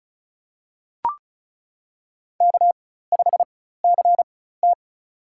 Morse code, heard as a keyed tone, about 36 wpm, spaced as for 16 wpm, 700 Hz. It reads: K5CT
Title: USA Call Signs - 4 Characters - 4x Character Spacing - 35wpm